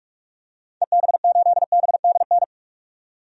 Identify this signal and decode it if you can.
Morse code, heard as a keyed tone, about 43 wpm, 700 Hz. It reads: EB8BDN
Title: DXCC-100 Call Signs - 6 Characters - 45wpm